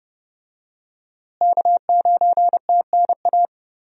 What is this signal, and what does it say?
Morse code, heard as a keyed tone, about 31 wpm, 700 Hz. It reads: K9TNA
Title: USA Call Signs - 5 Characters - Repeated 3x - 30wpm